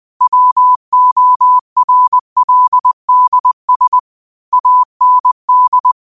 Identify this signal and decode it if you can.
Morse code, heard as a keyed tone, about 20 wpm, 1000 Hz. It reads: WORLDS AND
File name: qotdfiles / 20wpm